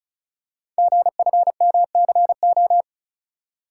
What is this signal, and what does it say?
Morse code, heard as a keyed tone, about 36 wpm, 690 Hz. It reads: GFMCO
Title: Code Groups - 5 Letters - 35wpm